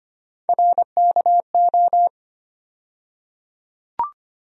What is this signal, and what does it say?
Morse code, heard as a keyed tone, about 25 wpm, 700 Hz. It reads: RKO
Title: USA Call Sign Postfixes - 3 Letters 25wpm